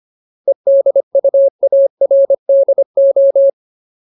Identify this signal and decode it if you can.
Morse code, heard as a keyed tone, about 25 wpm, 550 Hz. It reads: EDUARDO
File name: Ultimate Word List / 25WPM Ultimate